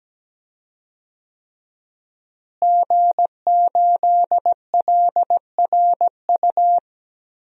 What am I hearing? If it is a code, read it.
Morse code, heard as a keyed tone, about 17 wpm, 700 Hz. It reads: G8LRU
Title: DXCC-100 Call Signs - 5 Characters - Repeated 3x - 17wpm